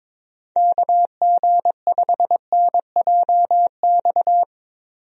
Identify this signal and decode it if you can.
Morse code, heard as a keyed tone, about 22 wpm, 700 Hz. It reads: KG5NJX
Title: USA Call Signs - 6 Characters - Repeated 2x - 22wpm